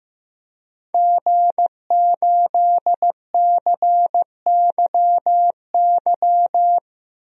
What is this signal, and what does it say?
Morse code, heard as a keyed tone, about 15 wpm, 700 Hz. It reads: G8CYY
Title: DXCC-100 Call Signs - 5 Characters - Repeated 3x - 15wpm